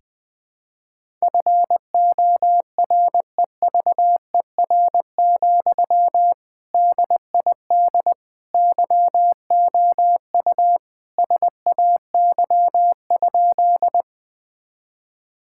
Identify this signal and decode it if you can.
Morse code, heard as a keyed tone, about 20 wpm, 700 Hz. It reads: FOREVER, DID YOU SAY?
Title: Sentences from Top 1600 Words 20wpm